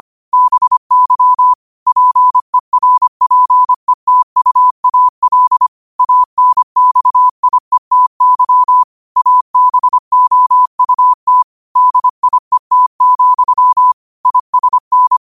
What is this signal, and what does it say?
Morse code, heard as a keyed tone, about 25 wpm, 1000 Hz. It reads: BY PERPETUAL ANXIETY ABOUT DIET, ISN
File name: qotdfiles / 25wpm